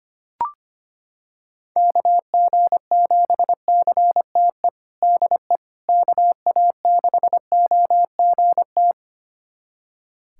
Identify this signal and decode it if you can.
Morse code, heard as a keyed tone, about 25 wpm, 700 Hz. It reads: KG7CTE DE KA6OGT
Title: USA Callsign DE Callsign - 6 Characters - 25wpm